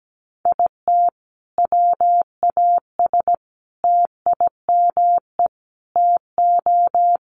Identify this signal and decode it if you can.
Morse code, heard as a keyed tone, about 17 wpm, 700 Hz. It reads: IT WAS TIME TO
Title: Sentences from Top 1100 Words 17wpm